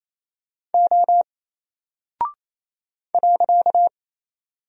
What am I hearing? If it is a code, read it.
Morse code, heard as a keyed tone, about 28 wpm, 700 Hz. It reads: O .
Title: Single Character (Letter:Number:Punctuation:Prosign) 28wpm